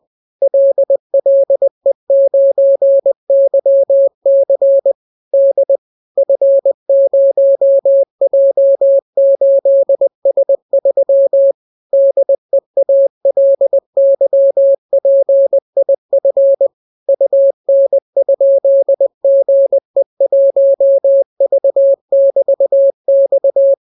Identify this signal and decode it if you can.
Morse code, heard as a keyed tone, about 20 wpm, 550 Hz. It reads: LLE9YC D F0J8S3 DEALYPIF UN?GE1V<BT>X